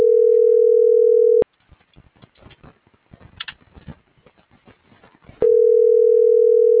Ringback tone (North America), 440 + 480 Hz.